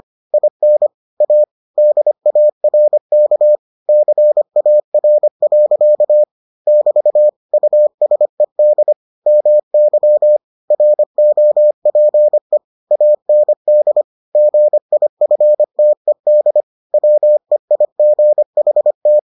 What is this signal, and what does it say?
Morse code, heard as a keyed tone, about 25 wpm, 600 Hz. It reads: IN A DARK CAR. <BT> USED MY ROPE AND GIFTED WEIGHT